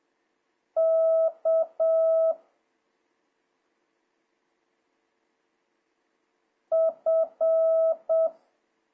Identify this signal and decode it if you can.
Morse code, heard as a keyed tone, about 7 wpm, 650 Hz. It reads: K F
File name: jh8cbh / 2026cw